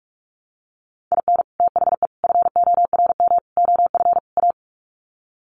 Morse code, heard as a keyed tone, about 44 wpm, 700 Hz. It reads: IN THE FORM OF A
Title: Sets of 5 Words 45wpm